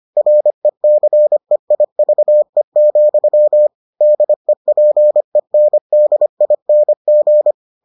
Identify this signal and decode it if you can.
Morse code, heard as a keyed tone, about 25 wpm, 600 Hz. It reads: RECEIVE, DEPENDING